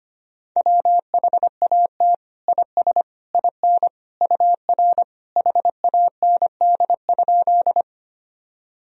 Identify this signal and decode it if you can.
Morse code, heard as a keyed tone, about 25 wpm, 700 Hz. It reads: WHAT IS IN UR HAND?
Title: Sentences from Top 200 Words and Abbreviations 25wpm